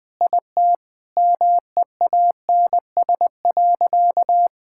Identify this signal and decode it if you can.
Morse code, heard as a keyed tone, about 20 wpm, 700 Hz. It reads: IT MEANS.
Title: Sentences with 5-Word Phrases - 20wpm